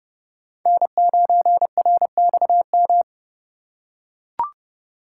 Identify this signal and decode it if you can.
Morse code, heard as a keyed tone, about 28 wpm, 700 Hz. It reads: N9RXM